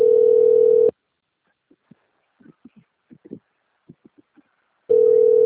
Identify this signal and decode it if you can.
Ringback tone (North America), 440 + 480 Hz.
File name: out- - - - - .13675